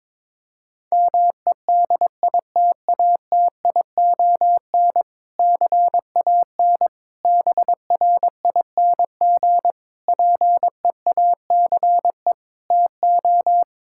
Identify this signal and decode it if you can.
Morse code, heard as a keyed tone, about 22 wpm, 700 Hz. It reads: MEDITATION CAN BRING PEACE TO
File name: Difficult Sentences with Hints - 22wpm